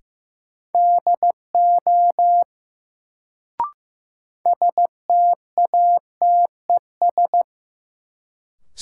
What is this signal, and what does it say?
Morse code, heard as a keyed tone, about 15 wpm, 700 Hz. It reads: DO STATES
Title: Sets of 2 Words - New Word Drill - 15wpm